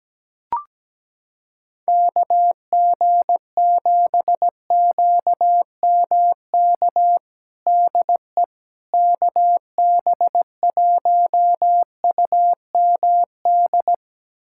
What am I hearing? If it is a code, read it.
Morse code, heard as a keyed tone, about 17 wpm, 700 Hz. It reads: KG7QMK DE KB1UMD
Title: USA Callsign DE Callsign - 6 Characters - 17wpm